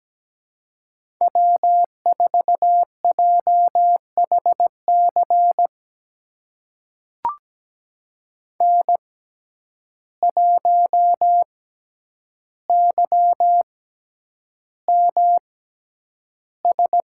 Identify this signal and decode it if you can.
Morse code, heard as a keyed tone, about 17 wpm, spaced as for 9 wpm, 700 Hz. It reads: W4JHC N1YMS
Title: USA Call Signs - 5 Characters - 6x Character Spacing - 17wpm